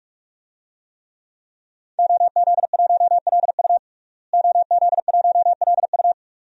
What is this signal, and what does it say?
Morse code, heard as a keyed tone, about 46 wpm, 700 Hz. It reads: OZ1LU OZ1LU
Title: DXCC-100 Call Signs - 5 Characters - Repeated 2x - 45wpm